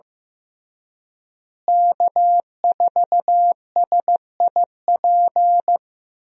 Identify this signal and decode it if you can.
Morse code, heard as a keyed tone, about 16 wpm, 700 Hz. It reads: K4SIP
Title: USA Call Signs - 5 Characters - Repeated 2x - 15wpm